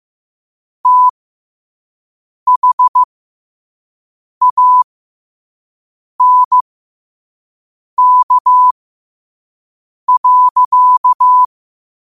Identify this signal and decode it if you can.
Morse code, heard as a keyed tone, about 15 wpm, spaced as for 5 wpm, 1000 Hz. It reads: THANK.
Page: Quotes for Fri, 15 Aug 2025 in Morse Code at 5 words per minute.